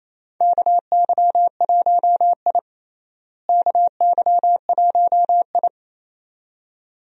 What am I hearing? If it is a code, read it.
Morse code, heard as a keyed tone, about 28 wpm, 700 Hz. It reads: KY1I KY1I
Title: USA Call Signs - 4 Characters - Repeated 2x - 28wpm